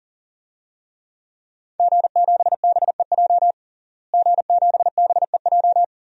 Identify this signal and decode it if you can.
Morse code, heard as a keyed tone, about 40 wpm, 700 Hz. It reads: G7BEJ G7BEJ
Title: DXCC-100 Call Signs - 5 Characters - Repeated 2x - 40wpm